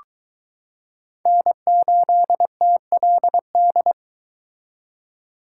Morse code, heard as a keyed tone, about 22 wpm, 700 Hz. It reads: N8TLD